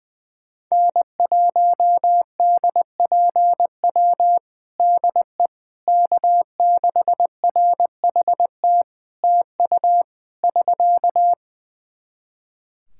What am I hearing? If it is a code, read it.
Morse code, heard as a keyed tone, about 20 wpm, 700 Hz. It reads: N1DPW DE K6RHT TU <SK>
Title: Practice QSO - Exchange 4 - Ending - 20wpm